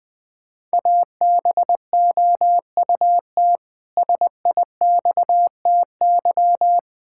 Morse code, heard as a keyed tone, about 20 wpm, 700 Hz. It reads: ABOUT SIXTY
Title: Pangram Challenge v2 - 20wpm